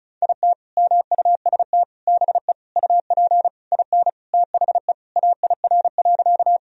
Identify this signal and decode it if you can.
Morse code, heard as a keyed tone, about 36 wpm, 700 Hz. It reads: IT MUST BE UP IN THE AIR.
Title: Sentences from Top 1000 Words - Review Part 1 - 35wpm